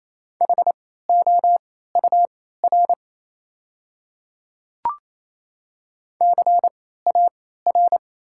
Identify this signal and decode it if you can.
Morse code, heard as a keyed tone, about 28 wpm, spaced as for 16 wpm, 690 Hz. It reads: HOUR CAR
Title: 3x Character Spacing - Top 500 Words - 28wpm